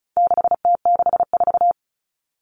Morse code, heard as a keyed tone, about 34 wpm, 700 Hz. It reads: 6T64
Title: All Cut Numbers - 4-Digit 35wpm